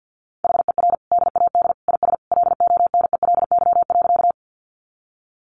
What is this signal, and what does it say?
Morse code, heard as a keyed tone, about 50 wpm, 700 Hz. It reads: HER DAD IS LONELY.
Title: Additional Easy Phrases 50wpm